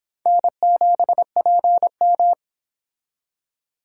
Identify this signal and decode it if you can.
Morse code, heard as a keyed tone, about 29 wpm, 700 Hz. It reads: N7PM